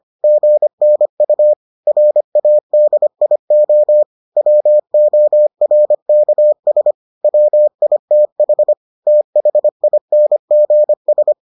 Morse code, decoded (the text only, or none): GNU RADIO WORKS WITH THINGS